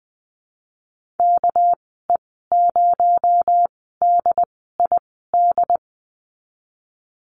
Morse code, heard as a keyed tone, about 20 wpm, spaced as for 14 wpm, 700 Hz. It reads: KE0DID
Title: 238 Lesson 38g - Letter q - Call Signs 20wpm